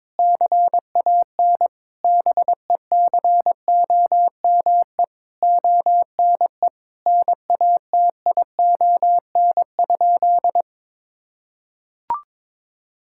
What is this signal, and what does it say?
Morse code, heard as a keyed tone, about 21 wpm, 700 Hz. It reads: CAN BECOME ONE NATION?
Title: Sentences from Top 1000 Words - Review Part 6 - 22wpm